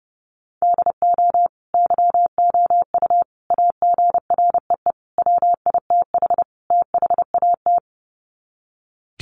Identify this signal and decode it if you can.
Morse code, heard as a keyed tone, about 28 wpm, 700 Hz. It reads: DO YOU AGREE WITH THAT
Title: Sets of 5 Words Encore 30wpm